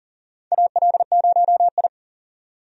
Morse code, heard as a keyed tone, about 40 wpm, 700 Hz. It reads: AL0I
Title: USA Call Signs - 4 Characters - 40wpm